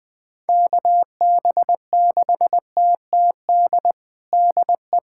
Morse code, heard as a keyed tone, about 21 wpm, 700 Hz. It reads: KB6TTD DE
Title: USA Callsign DE Callsign - 6 Characters - 20wpm